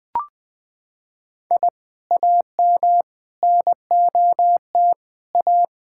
Morse code, heard as a keyed tone, about 21 wpm, 700 Hz. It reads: I AM NOT A
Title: Sentences from Top 1000 Words - Review Part 6 - 20wpm